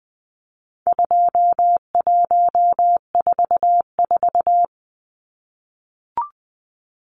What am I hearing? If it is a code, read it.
Morse code, heard as a keyed tone, about 20 wpm, 700 Hz. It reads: 2144
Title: Numbers - 4-Digit 20wpm